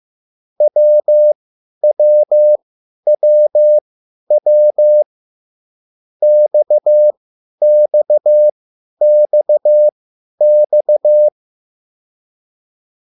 Morse code, heard as a keyed tone, about 15 wpm, spaced as for 10 wpm, 600 Hz. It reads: WWWW XXXX
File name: QSL / CW lesson